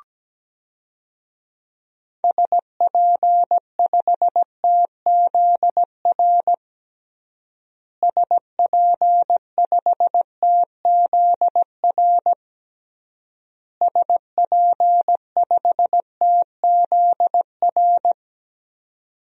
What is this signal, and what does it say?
Morse code, heard as a keyed tone, about 17 wpm, 700 Hz. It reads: SP5TZR SP5TZR SP5TZR
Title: DXCC-100 Call Signs - 6 Characters - Repeated 3x - 17wpm